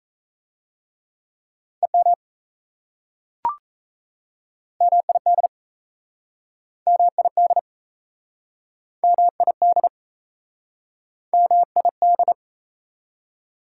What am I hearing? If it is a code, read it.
Morse code, heard as a keyed tone, about 37 wpm, 700 Hz. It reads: EM MID MID MID MID